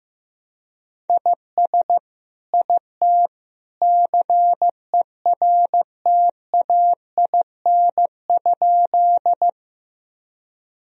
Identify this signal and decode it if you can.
Morse code, heard as a keyed tone, about 15 wpm, 700 Hz. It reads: IS IT CERTAIN?
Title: Sentences from Top 1000 Words - Review Part 2 - 15wpm